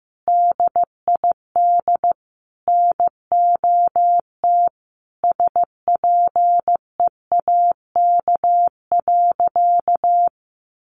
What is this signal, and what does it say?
Morse code, heard as a keyed tone, about 15 wpm, 700 Hz. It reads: DID NOT SPEAK.
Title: Sentences from Top 1500 Words 15wpm